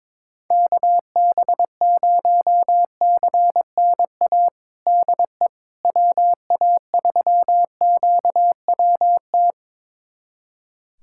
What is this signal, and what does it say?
Morse code, heard as a keyed tone, about 22 wpm, 700 Hz. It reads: KB0CNA DE WA3QWT
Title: USA Callsign DE Callsign - 6 Characters - 22wpm